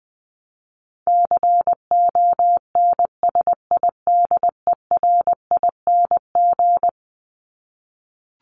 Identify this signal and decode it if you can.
Morse code, heard as a keyed tone, about 20 wpm, 700 Hz. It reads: CONSIDERING
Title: ING Suffix - Top 100 Words with Base Words 20wpm